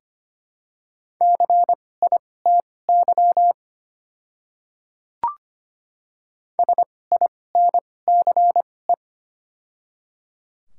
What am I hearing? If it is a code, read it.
Morse code, heard as a keyed tone, about 25 wpm, spaced as for 18 wpm, 690 Hz. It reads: CITY SINCE
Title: 2x Character Spacing - Top 300 Words - 25wpm